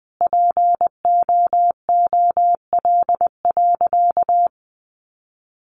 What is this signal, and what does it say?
Morse code, heard as a keyed tone, about 20 wpm, 700 Hz. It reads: POOL.
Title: Additional Easy Phrases 20wpm